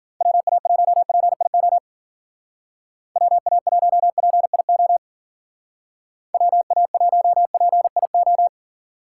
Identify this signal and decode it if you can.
Morse code, heard as a keyed tone, about 51 wpm, 700 Hz. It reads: WA1PIO WA1PIO WA1PIO